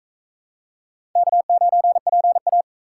Morse code, heard as a keyed tone, about 44 wpm, 700 Hz. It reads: K9PA